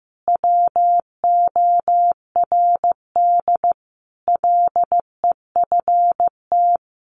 Morse code, heard as a keyed tone, about 15 wpm, 700 Hz. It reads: WORD LEFT
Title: Sentences from Top 1500 Words 15wpm